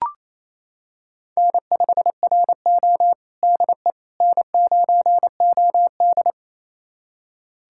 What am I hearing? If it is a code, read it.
Morse code, heard as a keyed tone, about 28 wpm, 690 Hz. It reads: N5RO DE N9OD
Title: USA Callsign DE Callsign - 4 Characters - 28wpm